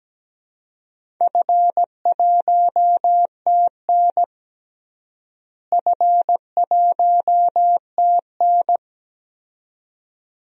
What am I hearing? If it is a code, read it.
Morse code, heard as a keyed tone, about 17 wpm, 700 Hz. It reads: F1TN F1TN
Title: DXCC-100 Call Signs - 4 Characters - Repeated 2x - 17wpm